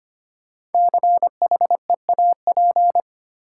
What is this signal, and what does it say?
Morse code, heard as a keyed tone, about 25 wpm, 700 Hz. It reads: CHEAP
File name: 5 Letter Words - First 3 Letters Given - 25wpm